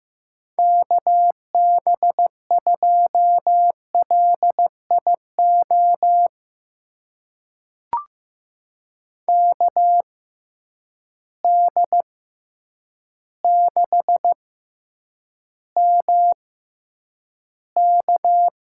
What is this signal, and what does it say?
Morse code, heard as a keyed tone, about 15 wpm, spaced as for 8 wpm, 700 Hz. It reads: KB2LIO KD6MK
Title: USA Call Signs - 6 Characters - 6x Character Spacing - 15wpm